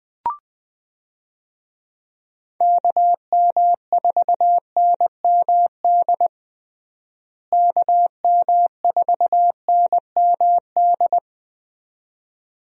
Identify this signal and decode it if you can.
Morse code, heard as a keyed tone, about 20 wpm, 700 Hz. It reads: KM4NMD KM4NMD
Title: USA Call Signs - 6 Characters - Repeated 2x - 20wpm